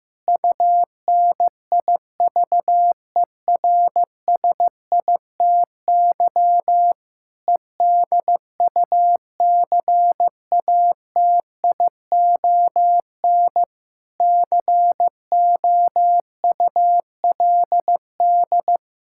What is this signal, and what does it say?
Morse code, heard as a keyed tone, about 15 wpm, 700 Hz. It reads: UNIVERSITY EDUCATION COULD
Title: Sentences from Top 1000 Words - Review Part 6 - 15wpm